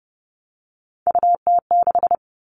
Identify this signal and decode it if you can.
Morse code, heard as a keyed tone, about 30 wpm, 700 Hz. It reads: UT6
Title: All Cut Numbers - 3-Digit 30wpm